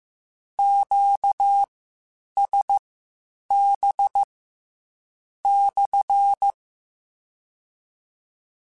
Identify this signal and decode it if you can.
Morse code, heard as a keyed tone, about 15 wpm, spaced as for 7 wpm, 800 Hz. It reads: QSB/
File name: Q Calls / Q Calls-mp3s / Q CALLS SPSL1 @13